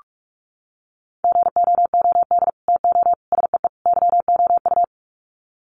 Morse code, heard as a keyed tone, about 44 wpm, 700 Hz. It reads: GOOD TO SEE YOU
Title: Sets of 4 Words 45wpm